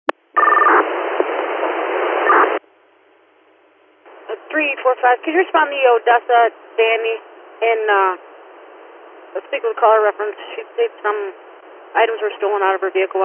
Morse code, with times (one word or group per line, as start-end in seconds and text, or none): none